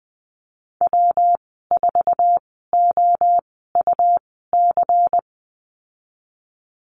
Morse code, heard as a keyed tone, about 20 wpm, spaced as for 15 wpm, 700 Hz. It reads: W4OUC
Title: 128 Lesson 21g - Number 6 - Call Signs 20wpm